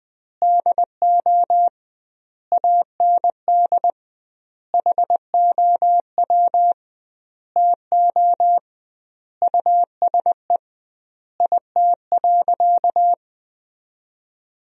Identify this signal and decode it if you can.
Morse code, heard as a keyed tone, about 20 wpm, 700 Hz. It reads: DO AND HOW TO USE IT.
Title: 2x Word-Spacing - Sentences from Top 500 Words - 20wpm